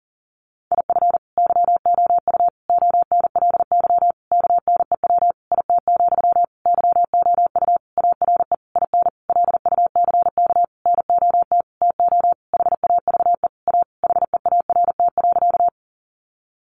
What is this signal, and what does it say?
Morse code, heard as a keyed tone, about 40 wpm, 700 Hz. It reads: IF YOU ONLY KNEW IT, YOU ARE IN LUCK NOT TO HAVE A HEART.
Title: Sentences from Top 1500 Words 40wpm